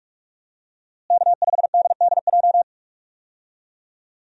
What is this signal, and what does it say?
Morse code, heard as a keyed tone, about 46 wpm, 700 Hz. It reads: K5DDJ
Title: USA Call Signs - 5 Characters - Repeated 2x - 45wpm